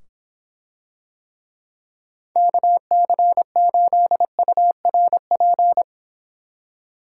Morse code, heard as a keyed tone, about 25 wpm, 700 Hz. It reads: KC8URP